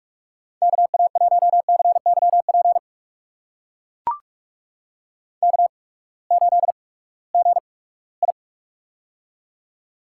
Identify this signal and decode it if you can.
Morse code, heard as a keyed tone, about 44 wpm, spaced as for 22 wpm, 700 Hz. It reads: KA1CYP K8GI
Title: USA Call Signs - 8x Character Spacing - 45wpm